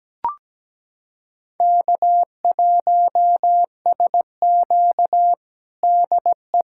Morse code, heard as a keyed tone, about 17 wpm, 700 Hz. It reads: K1SQ DE
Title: USA Callsign DE Callsign - 4 Characters - 17wpm